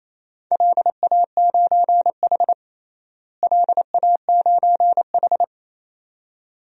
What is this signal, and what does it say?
Morse code, heard as a keyed tone, about 28 wpm, 700 Hz. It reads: LA9H LA9H
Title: DXCC-100 Call Signs - 4 Characters - Repeated 2x - 28wpm